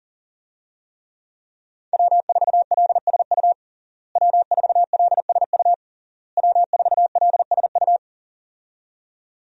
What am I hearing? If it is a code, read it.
Morse code, heard as a keyed tone, about 37 wpm, 700 Hz. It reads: W4LSU W4LSU W4LSU
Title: USA Call Signs - 5 Characters - Repeated 3x - 40wpm